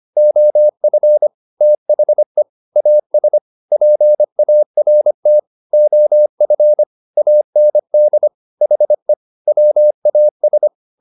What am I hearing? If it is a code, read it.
Morse code, heard as a keyed tone, about 25 wpm, 600 Hz. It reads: OF THE AS PART OF AND HE WAS
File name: WD204a 25